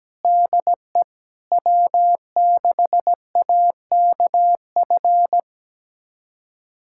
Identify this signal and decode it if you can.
Morse code, heard as a keyed tone, about 17 wpm, 700 Hz. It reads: DE W6AKF
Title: USA Callsign DE Callsign - 5 Characters - 17wpm